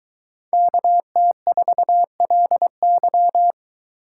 Morse code, heard as a keyed tone, about 21 wpm, 700 Hz. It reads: KT4LY